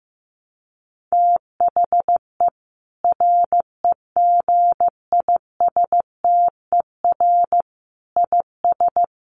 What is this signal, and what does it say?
Morse code, heard as a keyed tone, about 15 wpm, 700 Hz. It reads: THE REGISTER IS
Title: Sentences from Top 1200 Words 15wpm